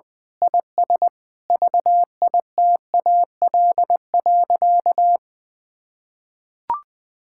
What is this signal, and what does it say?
Morse code, heard as a keyed tone, about 20 wpm, 700 Hz. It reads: IS VITAL.
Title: Sentences with 8-Word Phrases - 20wpm